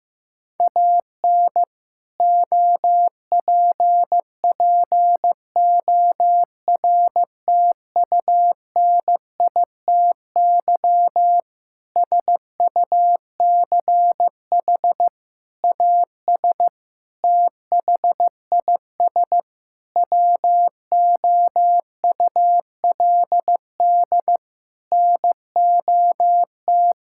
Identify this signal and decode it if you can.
Morse code, heard as a keyed tone, about 15 wpm, 700 Hz. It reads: AN OPPORTUNITY SUCH AS THIS WOULD NOT
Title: Sentences from Top 1600 Words 15wpm